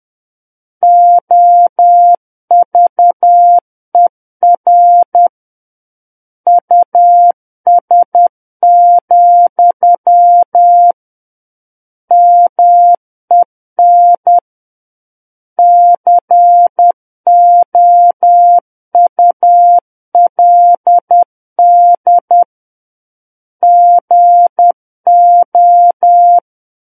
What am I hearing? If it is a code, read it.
Morse code, heard as a keyed tone, about 10 wpm, 700 Hz. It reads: OVER US, MEN COULD GO
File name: Book One / 01-Chapter 1 - 10 WPM